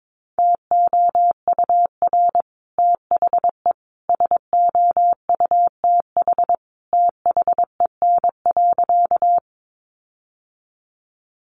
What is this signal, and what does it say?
Morse code, heard as a keyed tone, about 22 wpm, 700 Hz. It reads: TOUR THE SOUTH THEN.
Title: Sentences from Top 1400 Words 22wpm